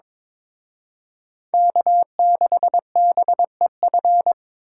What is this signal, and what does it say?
Morse code, heard as a keyed tone, about 25 wpm, 700 Hz. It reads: K6BEF
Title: Speed Racing - USA 1x3 Call Signs 22wpm